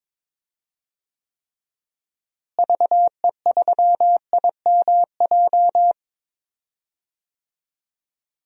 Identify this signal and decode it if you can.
Morse code, heard as a keyed tone, about 22 wpm, 700 Hz. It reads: VE3IMJ
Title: DXCC-100 Call Signs - 6 Characters - Repeated 2x - 22wpm